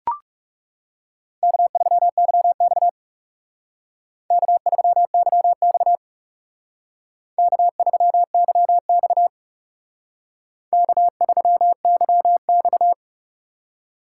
Morse code, heard as a keyed tone, about 34 wpm, 700 Hz. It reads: K3YX K3YX K3YX K3YX
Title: Speed Racing - USA 1x2 Call Signs 30wpm